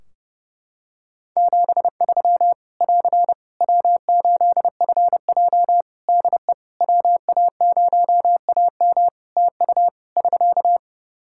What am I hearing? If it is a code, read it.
Morse code, heard as a keyed tone, about 28 wpm, 700 Hz. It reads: 73 <AR> W8FJ DE WA0AM TU <SK>
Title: Practice QSO - Exchange 4 - Ending - 30wpm